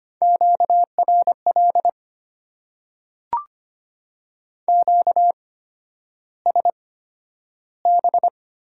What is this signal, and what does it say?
Morse code, heard as a keyed tone, about 25 wpm, spaced as for 11 wpm, 690 Hz. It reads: QRL QSB
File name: Top 90 QSO Elements - 8x Character Spacing - 25wpm